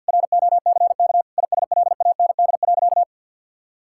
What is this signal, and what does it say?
Morse code, heard as a keyed tone, about 50 wpm, 700 Hz. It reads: ROCK ISLAND.
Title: Sentences from Top 1600 Words 50wpm